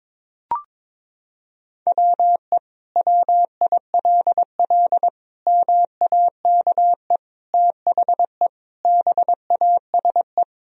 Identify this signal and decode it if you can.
Morse code, heard as a keyed tone, about 22 wpm, 700 Hz. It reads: WE WILL MAKE THE BASE
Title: Sentences from Top 1000 Words - Review Part 1 - 22wpm